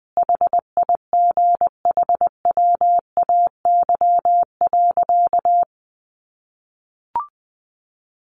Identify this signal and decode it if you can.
Morse code, heard as a keyed tone, about 20 wpm, 700 Hz. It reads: HIGHWAY.
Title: Sentences from Top 1400 Words 20wpm